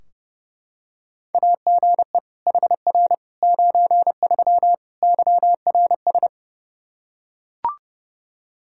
Morse code, heard as a keyed tone, about 30 wpm, 700 Hz. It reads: AGE HR 93 YRS
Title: Practice QSO - Exchange 3 - More Info - 30wpm